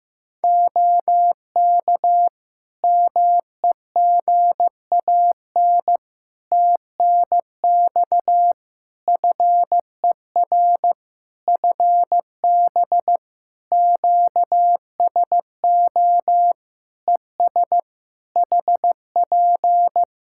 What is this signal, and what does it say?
Morse code, heard as a keyed tone, about 15 wpm, 700 Hz. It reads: OK MEGAN TNX FER FB QSO ES HP
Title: Practice QSO - Exchange 4 - Ending - 15wpm